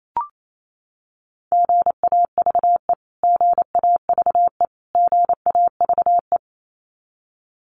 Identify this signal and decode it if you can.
Morse code, heard as a keyed tone, about 28 wpm, 700 Hz. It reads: GAVE GAVE GAVE
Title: Top 500 Words Repeated 28wpm